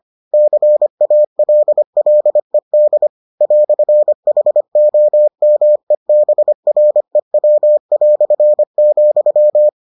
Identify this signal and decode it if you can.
Morse code, heard as a keyed tone, about 25 wpm, 600 Hz. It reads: CALLED "HOMEBREW",